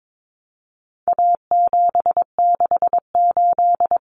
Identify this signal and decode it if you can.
Morse code, heard as a keyed tone, about 22 wpm, 700 Hz. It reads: A768
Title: Common Cut Numbers - 4-Digit 22wpm